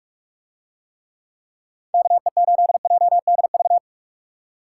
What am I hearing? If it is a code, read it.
Morse code, heard as a keyed tone, about 44 wpm, 700 Hz. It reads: KE8JDV